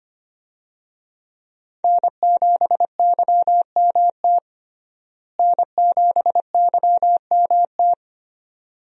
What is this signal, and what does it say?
Morse code, heard as a keyed tone, about 25 wpm, 700 Hz. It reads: N7YMT N7YMT
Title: USA Call Signs - 5 Characters - Repeated 2x - 25wpm